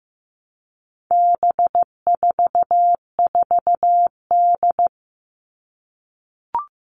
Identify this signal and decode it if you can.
Morse code, heard as a keyed tone, about 15 wpm, 700 Hz. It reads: B44D
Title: All Cut Numbers - 4-Digit 15wpm